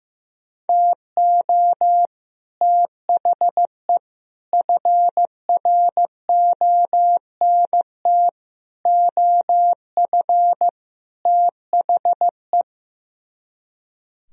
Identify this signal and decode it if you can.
Morse code, heard as a keyed tone, about 15 wpm, 700 Hz. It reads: TO THE FRONT OF THE
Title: Speed-Racing - Sets of 5 Words 15wpm